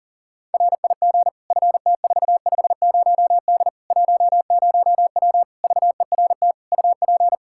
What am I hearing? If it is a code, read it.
Morse code, heard as a keyed tone, about 40 wpm, 700 Hz. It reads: RIG FT450D 10W VERT UP
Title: Practice QSO - Exchange 2 - Info - 40wpm